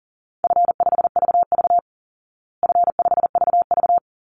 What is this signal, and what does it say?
Morse code, heard as a keyed tone, about 37 wpm, 700 Hz. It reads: F5VV F5VV
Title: Easy Call Signs Repeated 2x - 40wpm